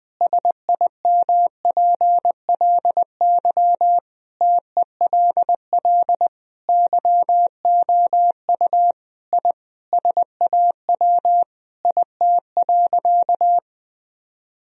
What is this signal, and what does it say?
Morse code, heard as a keyed tone, about 20 wpm, 700 Hz. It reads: SIMPLY TELL YOU I SAW IT.
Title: Sentences from Top 1000 Words - Review Part 4 - 20wpm